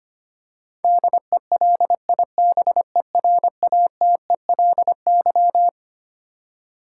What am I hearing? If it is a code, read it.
Morse code, heard as a keyed tone, about 25 wpm, 700 Hz. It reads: DELIBERATELY
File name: Twelve Letter Words - 25wpm